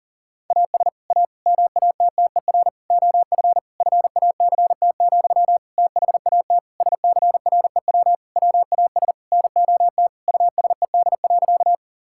Morse code, heard as a keyed tone, about 40 wpm, 700 Hz. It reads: AS A MATTER OF FACT, THAT SCREW WAS NOT USED.
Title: Sentences from Top 1600 Words 40wpm